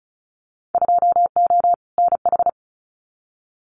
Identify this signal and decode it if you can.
Morse code, heard as a keyed tone, about 34 wpm, 700 Hz. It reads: 2O NH
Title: Winter Field Day Exchanges 35wpm